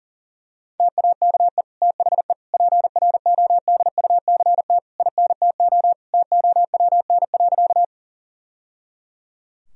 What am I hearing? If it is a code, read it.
Morse code, heard as a keyed tone, about 40 wpm, 700 Hz. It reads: TAKE THE PRODUCT INTO TOWN.
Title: Sentences from Top 1000 Words - Review Part 2 - 40wpm